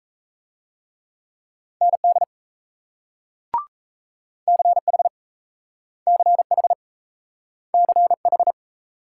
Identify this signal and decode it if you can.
Morse code, heard as a keyed tone, about 38 wpm, 700 Hz. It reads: ND CH CH CH